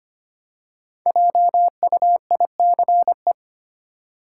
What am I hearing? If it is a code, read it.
Morse code, heard as a keyed tone, about 25 wpm, 690 Hz. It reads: JUICE
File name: practice / 25wpm / Easy Words with Hints - 25wpm